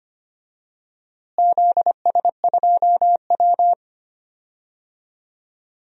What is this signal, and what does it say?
Morse code, heard as a keyed tone, about 25 wpm, 700 Hz. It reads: ZS2W
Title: DXCC-100 Call Signs - 4 Characters - Repeated 2x - 25wpm